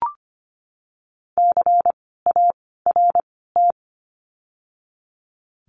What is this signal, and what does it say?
Morse code, heard as a keyed tone, about 25 wpm, 700 Hz. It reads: C A R T
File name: course / 25:10wpm / 094 Lesson 16f - Letter c - Words 25wpm